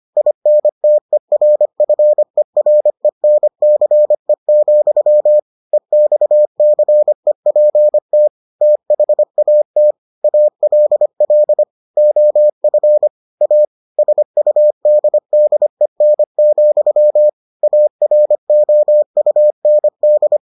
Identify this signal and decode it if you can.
Morse code, heard as a keyed tone, about 25 wpm, 600 Hz. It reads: INTERFERENCE, EXCEPT THAT ALL OF A SUDDEN, AROUND